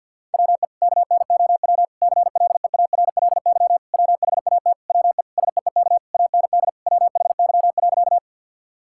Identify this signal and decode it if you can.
Morse code, heard as a keyed tone, about 50 wpm, 700 Hz. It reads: WE KNOW CLEARLY WHAT WE SEEK AND WHY.
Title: Sentences from Top 1600 Words 50wpm